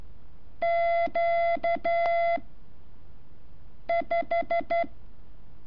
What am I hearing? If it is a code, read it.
Morse code, heard as a keyed tone, about 12 wpm, 680 Hz. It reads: Q 5
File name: curso / ejercicio 16